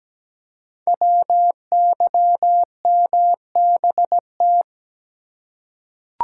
Morse code, heard as a keyed tone, about 17 wpm, 700 Hz. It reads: WYMBT
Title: Code Groups - 5 Letters - 17wpm